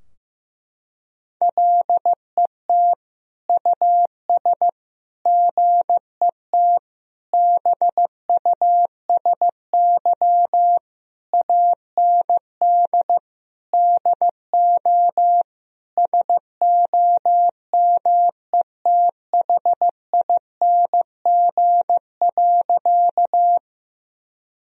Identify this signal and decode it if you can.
Morse code, heard as a keyed tone, about 16 wpm, 700 Hz. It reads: LET US GET BUSY AND DO SOMETHING.
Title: Sentences from Top 1000 Words - Review Part 3 - 15wpm